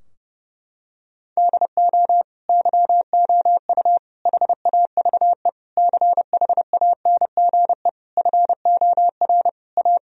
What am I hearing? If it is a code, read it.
Morse code, heard as a keyed tone, about 30 wpm, 700 Hz. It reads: DO YOU HAVE CHANGE FOR A
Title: Sentences from Top 1000 Words - Review Part 2 - 30wpm